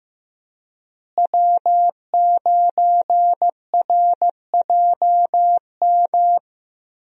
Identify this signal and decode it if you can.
Morse code, heard as a keyed tone, about 15 wpm, 700 Hz. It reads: W9RJM
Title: USA Call Signs - 5 Characters - Repeated 2x - 15wpm